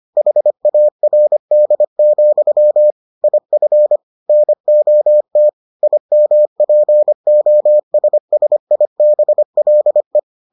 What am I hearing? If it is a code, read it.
Morse code, heard as a keyed tone, about 25 wpm, 600 Hz. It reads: HARD, IF NOT IMPOSSIBLE